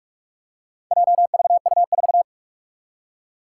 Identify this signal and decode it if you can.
Morse code, heard as a keyed tone, about 44 wpm, 700 Hz. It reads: JVU4
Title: Four Challenging Letter-Number Combinations 45wpm